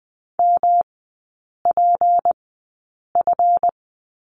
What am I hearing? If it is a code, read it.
Morse code, heard as a keyed tone, about 20 wpm, spaced as for 8 wpm, 690 Hz. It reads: MPF
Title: 212 Lesson 34e - Number 8 - Three Characters 20wpm